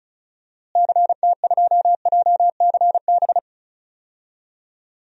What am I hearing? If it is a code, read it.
Morse code, heard as a keyed tone, about 34 wpm, 700 Hz. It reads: CT2JCB
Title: DXCC-100 Call Signs - 6 Characters - Repeated 2x - 35wpm